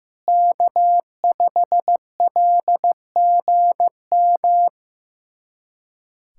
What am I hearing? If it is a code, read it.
Morse code, heard as a keyed tone, about 16 wpm, 700 Hz. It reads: K5LGM
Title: USA Callsign DE Callsign 15wpm